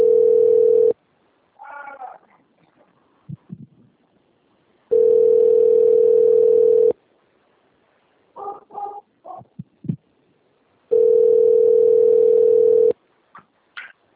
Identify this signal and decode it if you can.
Ringback tone (North America), 440 + 480 Hz.
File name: out- - - - - .8434